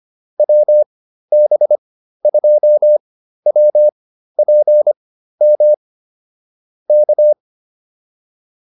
Morse code, heard as a keyed tone, about 25 wpm, spaced as for 13 wpm, 600 Hz. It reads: WB2WPM K
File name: QSO106 13